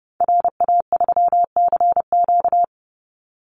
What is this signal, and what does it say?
Morse code, heard as a keyed tone, about 30 wpm, 700 Hz. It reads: RA3CQ
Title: Call Signs Repeated 3x - 30wpm